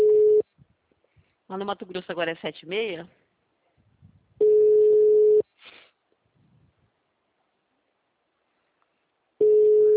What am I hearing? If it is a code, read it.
Ringback tone (Europe, 425 Hz), with speech.